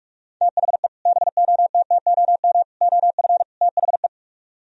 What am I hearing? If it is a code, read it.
Morse code, heard as a keyed tone, about 45 wpm, 700 Hz. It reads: THE BOTTOM OF THE